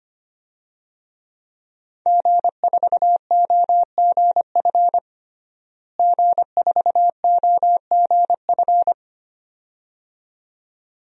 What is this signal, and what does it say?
Morse code, heard as a keyed tone, about 25 wpm, 700 Hz. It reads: G4OGF G4OGF
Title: DXCC-100 Call Signs - 5 Characters - Repeated 2x - 25wpm